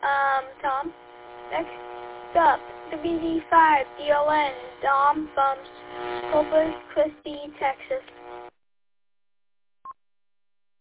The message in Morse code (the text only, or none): none